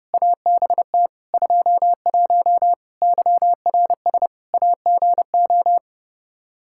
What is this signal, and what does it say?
Morse code, heard as a keyed tone, about 30 wpm, 700 Hz. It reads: ABT 21 YRS AGO
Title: Practice QSO - Exchange 3 - More Info - 30wpm